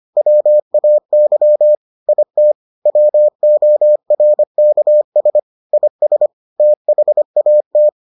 Morse code, heard as a keyed tone, about 25 wpm, 600 Hz. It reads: WAY IT WORKS IS THAT